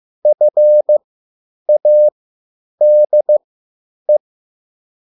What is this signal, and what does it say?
Morse code, heard as a keyed tone, about 15 wpm, spaced as for 9 wpm, 600 Hz. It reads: FADE
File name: QSL / CW lesson